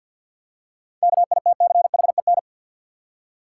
Morse code, heard as a keyed tone, about 53 wpm, 700 Hz. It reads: KITCHEN
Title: Seven Letter Words - 50wpm